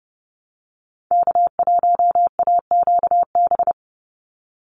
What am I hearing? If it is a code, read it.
Morse code, heard as a keyed tone, about 30 wpm, 700 Hz. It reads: K1AQB
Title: Call Signs Repeated 3x - 30wpm